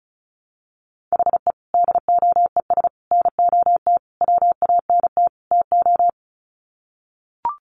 Morse code, heard as a keyed tone, about 34 wpm, 700 Hz. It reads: HE DOES NOT WANT TO
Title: Sets of 5 Words 35wpm